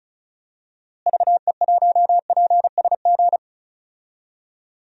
Morse code, heard as a keyed tone, about 34 wpm, 700 Hz. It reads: VE1PSG
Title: DXCC-100 Call Signs - 6 Characters - Repeated 3x - 35wpm